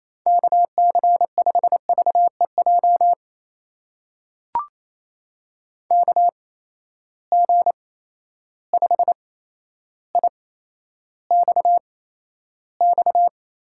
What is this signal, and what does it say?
Morse code, heard as a keyed tone, about 28 wpm, spaced as for 12 wpm, 700 Hz. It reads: KC5VEJ KG5IXX
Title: USA Call Signs - 6 Characters - 8x Character Spacing - 28wpm